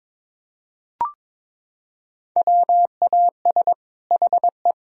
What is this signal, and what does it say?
Morse code, heard as a keyed tone, about 21 wpm, 700 Hz. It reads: WAS HE
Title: Sentences from Top 1000 Words - Review Part 4 - 22wpm